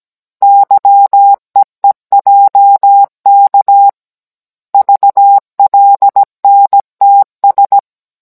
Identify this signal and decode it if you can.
Morse code, heard as a keyed tone, about 17 wpm, 800 Hz. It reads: YEEJK VLNTS